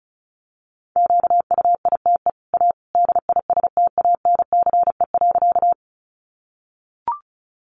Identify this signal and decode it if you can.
Morse code, heard as a keyed tone, about 34 wpm, 700 Hz. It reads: QUITE A DISTANCE.
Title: Sentences from Top 1200 Words 35wpm